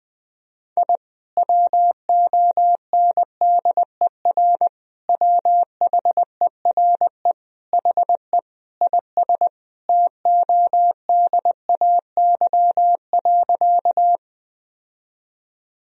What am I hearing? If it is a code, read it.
Morse code, heard as a keyed tone, about 21 wpm, 700 Hz. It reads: I WONDER WHERE HE IS TODAY.
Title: Sentences from Top 1000 Words - Review Part 3 - 20wpm